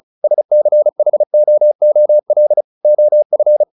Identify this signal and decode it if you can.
Morse code, heard as a keyed tone, about 35 wpm, 600 Hz. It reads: SCHOOL OF
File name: the old man and the sea / the old man and the sea 1 35wpm